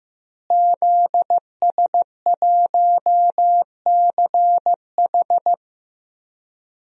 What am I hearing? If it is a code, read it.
Morse code, heard as a keyed tone, about 15 wpm, 700 Hz. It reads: ZS1CH
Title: DXCC-100 Call Signs - 5 Characters - Repeated 3x - 15wpm